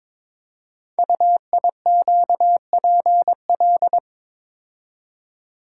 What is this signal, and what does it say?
Morse code, heard as a keyed tone, about 22 wpm, 700 Hz. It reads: UIQPL
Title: Code Groups - 5 Letters - 22wpm